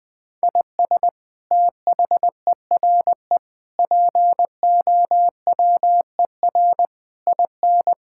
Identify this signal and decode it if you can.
Morse code, heard as a keyed tone, about 20 wpm, 700 Hz. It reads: IS THERE POWER IN
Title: Sentences from Top 1000 Words - Review Part 6 - 20wpm